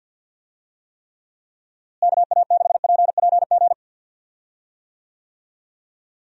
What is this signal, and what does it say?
Morse code, heard as a keyed tone, about 47 wpm, 700 Hz. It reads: KA6PPG
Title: USA Call Signs - 6 Characters - Repeated 2x - 50wpm